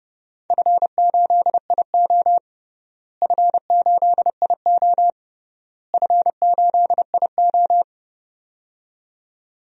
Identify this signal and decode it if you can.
Morse code, heard as a keyed tone, about 30 wpm, 700 Hz. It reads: F8IO F8IO F8IO
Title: DXCC-100 Call Signs - 4 Characters - Repeated 3x - 30wpm